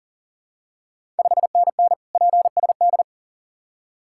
Morse code, heard as a keyed tone, about 40 wpm, 690 Hz. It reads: 5NN PSD
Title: Ontario QSO Party Exchange 40wpm